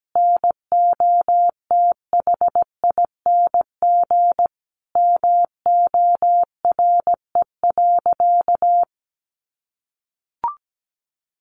Morse code, heard as a keyed tone, about 17 wpm, 700 Hz. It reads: NOTHING MORE.
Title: Sentences from Top 800 Words 17wpm